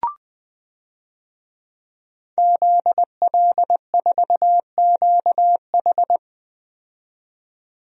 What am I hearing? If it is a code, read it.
Morse code, heard as a keyed tone, about 21 wpm, 700 Hz. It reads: ZL4QH
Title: DXCC-100 Call Signs - 5 Characters - 20wpm